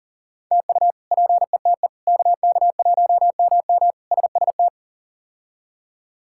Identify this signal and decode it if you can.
Morse code, heard as a keyed tone, about 40 wpm, 700 Hz. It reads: TU PETE KK1MM SST
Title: K1USN SST QSO Practice - 40wpm